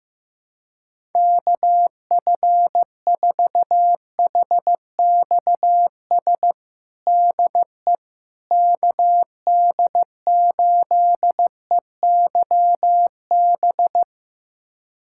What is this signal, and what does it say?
Morse code, heard as a keyed tone, about 15 wpm, 700 Hz. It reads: KF4HXS DE KD8EYB
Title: USA Callsign DE Callsign - 6 Characters - 15wpm